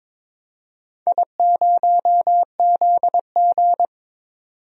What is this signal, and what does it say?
Morse code, heard as a keyed tone, about 22 wpm, 700 Hz. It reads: I0ZG
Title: DXCC-100 Call Signs - 4 Characters - 22wpm